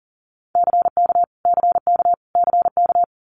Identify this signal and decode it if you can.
Morse code, heard as a keyed tone, about 39 wpm, 700 Hz. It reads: CK CK CK
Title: Multi-letter Phonemes Repeated 3x 40wpm